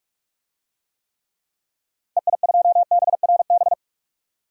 Morse code, heard as a keyed tone, about 46 wpm, 700 Hz. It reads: EI2BRB
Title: DXCC-100 Call Signs - 6 Characters - 45wpm